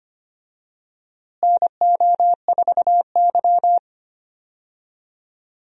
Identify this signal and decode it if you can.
Morse code, heard as a keyed tone, about 25 wpm, 700 Hz. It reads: NO4Y
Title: USA Call Signs - 4 Characters - Repeated 2x - 25wpm